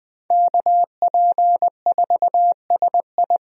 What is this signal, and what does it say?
Morse code, heard as a keyed tone, about 21 wpm, 700 Hz. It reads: KP4SI
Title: USA Call Signs - 5 Characters - 20wpm